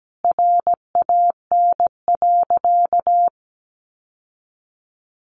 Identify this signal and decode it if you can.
Morse code, heard as a keyed tone, about 17 wpm, 700 Hz. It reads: RAN.
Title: Sentences from Top 600 Words 17wpm